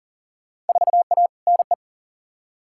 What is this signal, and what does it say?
Morse code, heard as a keyed tone, about 39 wpm, 700 Hz. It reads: 4A NE
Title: Field Day Exchanges - 40wpm